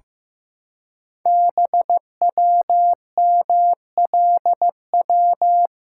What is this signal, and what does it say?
Morse code, heard as a keyed tone, about 15 wpm, 700 Hz. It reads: BWMLW
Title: Code Groups - 5 Letters - 15wpm